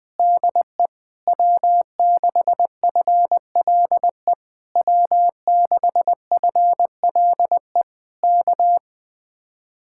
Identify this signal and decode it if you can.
Morse code, heard as a keyed tone, about 20 wpm, 700 Hz. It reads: DE W6FLE W6FLE K
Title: Speed-Racing Common QSO Phrases Encore - 17wpm